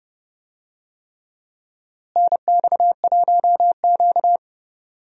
Morse code, heard as a keyed tone, about 30 wpm, 700 Hz. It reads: NX1Q
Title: USA Call Signs - 4 Characters - Repeated 3x - 30wpm